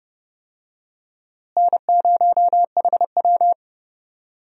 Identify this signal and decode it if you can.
Morse code, heard as a keyed tone, about 30 wpm, 700 Hz. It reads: N0HW
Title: USA Call Signs - 4 Characters - Repeated 2x - 30wpm